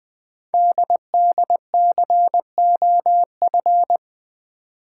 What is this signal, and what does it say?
Morse code, heard as a keyed tone, about 20 wpm, 700 Hz. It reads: DDCOF
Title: Code Groups - 5 Letters - 20wpm